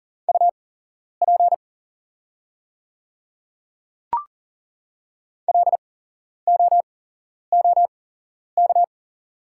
Morse code, heard as a keyed tone, about 40 wpm, spaced as for 11 wpm, 700 Hz. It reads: UP LOOK
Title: 8x Character Spacing - Top 100 Words - 40wpm